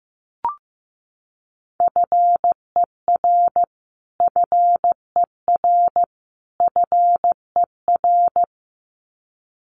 Morse code, heard as a keyed tone, about 15 wpm, 700 Hz. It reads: FER FER FER
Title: Abbreviations Repeated 3x - 15wpm